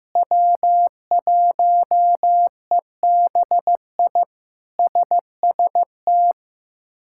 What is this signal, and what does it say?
Morse code, heard as a keyed tone, about 16 wpm, 700 Hz. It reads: W1EBI SST
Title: K1USN SST QSO Practice - 15wpm